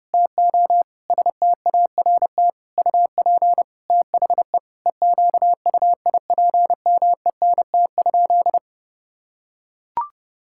Morse code, heard as a keyed tone, about 30 wpm, 700 Hz. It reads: TO START UP THE EQUIPMENT?
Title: Sentences from Top 1000 Words - Review Part 6 - 30wpm